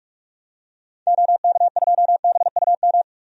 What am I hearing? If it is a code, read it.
Morse code, heard as a keyed tone, about 46 wpm, 700 Hz. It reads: OK2BUM
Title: DXCC-100 Call Signs - 6 Characters - 45wpm